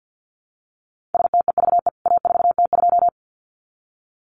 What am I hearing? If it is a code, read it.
Morse code, heard as a keyed tone, about 50 wpm, 700 Hz. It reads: STEVE A4T2
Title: CWT Exchange 50wpm